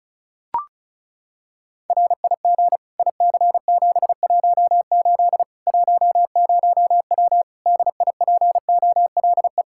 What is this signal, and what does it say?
Morse code, heard as a keyed tone, about 36 wpm, 700 Hz. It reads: RIG IC718 10W DIPOLE
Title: Practice QSO - Exchange 2 - Info - 35wpm